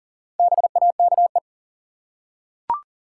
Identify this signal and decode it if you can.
Morse code, heard as a keyed tone, about 40 wpm, 700 Hz. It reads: BAKE
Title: Four Letter Words - 40wpm